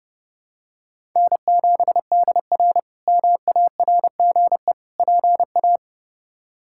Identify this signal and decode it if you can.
Morse code, heard as a keyed tone, about 30 wpm, 700 Hz. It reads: N7DR MARGE PA
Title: K1USN SST QSO Practice - 30wpm